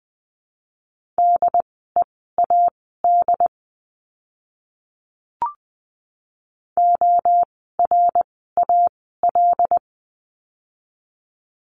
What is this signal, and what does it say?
Morse code, heard as a keyed tone, about 20 wpm, spaced as for 14 wpm, 700 Hz. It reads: DEAD ORAL
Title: 075 Lesson 13f - Letter L - Words 20wpm